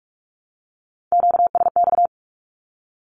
Morse code, heard as a keyed tone, about 44 wpm, 700 Hz. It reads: QSX
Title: Q-Codes Repeated 3x 45wpm